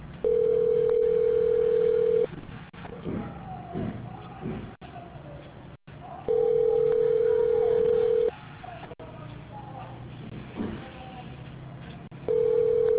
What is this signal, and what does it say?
Ringback tone (North America), 440 + 480 Hz.